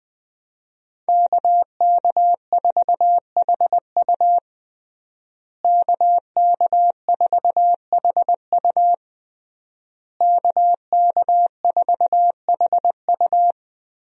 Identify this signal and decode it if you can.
Morse code, heard as a keyed tone, about 20 wpm, 700 Hz. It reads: KK4HU KK4HU KK4HU
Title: USA Call Signs - 5 Characters - Repeated 3x - 20wpm